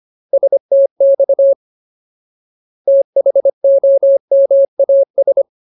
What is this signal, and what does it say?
Morse code, heard as a keyed tone, about 25 wpm, 550 Hz. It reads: STX THOMAS
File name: Ultimate Word List / 25WPM Ultimate